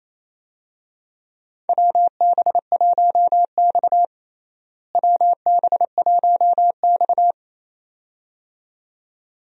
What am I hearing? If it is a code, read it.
Morse code, heard as a keyed tone, about 28 wpm, 700 Hz. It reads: WB1X WB1X
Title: USA Call Signs - 4 Characters - Repeated 2x - 28wpm